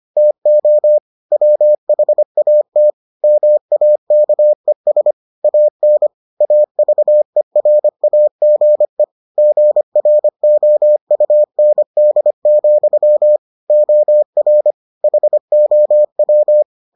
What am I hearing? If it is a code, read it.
Morse code, heard as a keyed tone, about 25 wpm, 600 Hz. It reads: TO WHAT MAKES AN AVERAGE GROUND, OR HOW